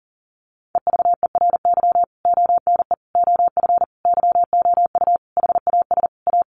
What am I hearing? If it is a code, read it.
Morse code, heard as a keyed tone, about 40 wpm, 700 Hz. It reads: EVERY ONE OF YOU HAS A
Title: Sentences from Top 1500 Words 40wpm